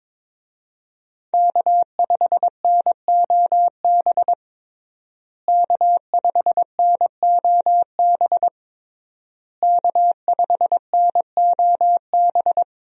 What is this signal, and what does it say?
Morse code, heard as a keyed tone, about 22 wpm, 700 Hz. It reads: K5NOB K5NOB K5NOB
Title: USA Call Signs - 5 Characters - Repeated 3x - 22wpm